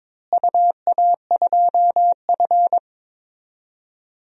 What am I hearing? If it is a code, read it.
Morse code, heard as a keyed tone, about 22 wpm, 700 Hz. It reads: UA2F
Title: DXCC-100 Call Signs - 4 Characters - Repeated 3x - 22wpm